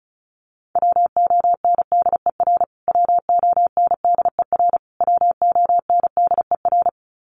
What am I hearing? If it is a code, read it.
Morse code, heard as a keyed tone, about 34 wpm, 700 Hz. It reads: WONDER WONDER WONDER
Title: Top 500 Words Repeated 35wpm